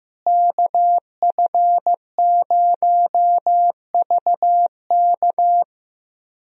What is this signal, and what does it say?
Morse code, heard as a keyed tone, about 15 wpm, 700 Hz. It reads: KF0VK
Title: USA Call Signs - 5 Characters - Repeated 3x - 15wpm